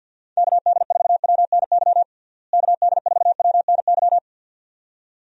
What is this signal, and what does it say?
Morse code, heard as a keyed tone, about 51 wpm, 700 Hz. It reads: KD4WNY KD4WNY
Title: USA Call Signs - 6 Characters - Repeated 2x - 50wpm